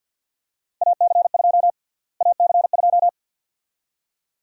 Morse code, heard as a keyed tone, about 52 wpm, 700 Hz. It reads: AC2 AC2
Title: USA Call Sign Prefixes - Repeated 2x - 50wpm